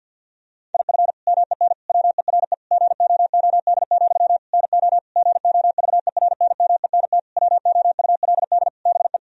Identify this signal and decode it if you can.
Morse code, heard as a keyed tone, about 51 wpm, 700 Hz. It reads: IF MEN WERE GOOD, NO GOVERNMENT WOULD BE
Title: Sentences from Top 1000 Words - Review Part 3 - 50wpm